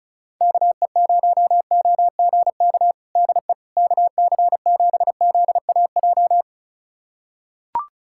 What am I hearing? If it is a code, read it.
Morse code, heard as a keyed tone, about 36 wpm, 700 Hz. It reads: KE0OGK DE KC7ZAJ
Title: USA Callsign DE Callsign - 6 Characters - 35wpm